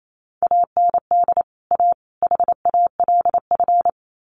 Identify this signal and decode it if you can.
Morse code, heard as a keyed tone, about 28 wpm, 700 Hz. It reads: AND A HALF
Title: Sets of 3 Words 28wpm